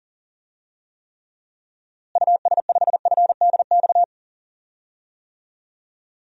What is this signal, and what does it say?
Morse code, heard as a keyed tone, about 40 wpm, 700 Hz. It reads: US5FDX
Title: DXCC-100 Call Signs - 6 Characters - Repeated 2x - 40wpm